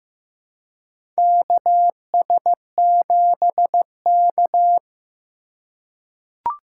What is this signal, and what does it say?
Morse code, heard as a keyed tone, about 15 wpm, 700 Hz. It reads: KS7K
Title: USA Call Signs - 4 Characters - 3x Character Spacing - 15wpm